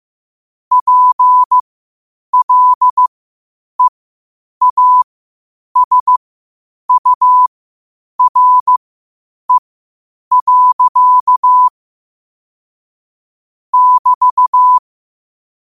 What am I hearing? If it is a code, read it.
Morse code, heard as a keyed tone, about 15 wpm, spaced as for 8 wpm, 1000 Hz. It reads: PLEASURE. <BT>
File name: qotdfiles / 8wpm